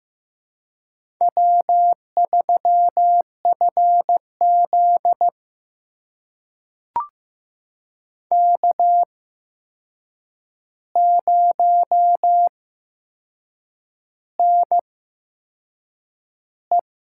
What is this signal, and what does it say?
Morse code, heard as a keyed tone, about 15 wpm, spaced as for 6 wpm, 700 Hz. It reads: W3FZ K0NE
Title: USA Call Signs - 4 Characters - 8x Character Spacing - 15wpm